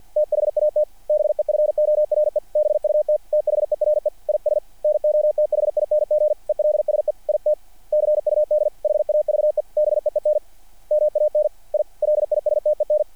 Morse code, heard as a keyed tone, about 48 wpm, 660 Hz. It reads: THAT BEFORE BUT THERE IS NOTHING ELSE IT CUD HAVE BEEN MAN I LISTEN
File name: archived / CRSnet